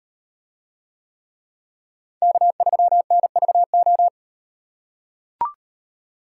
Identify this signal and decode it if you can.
Morse code, heard as a keyed tone, about 38 wpm, 700 Hz. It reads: K3NVO